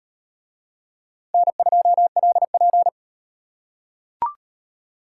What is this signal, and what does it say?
Morse code, heard as a keyed tone, about 38 wpm, 700 Hz. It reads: N2LP